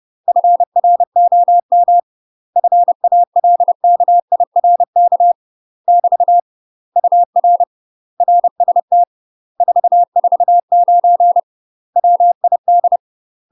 Morse code, heard as a keyed tone, about 30 wpm, 700 Hz. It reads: FROM FALKIRK <BT> UR RST 449 WID
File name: UK MASTER QSOs TRAINING/30WPM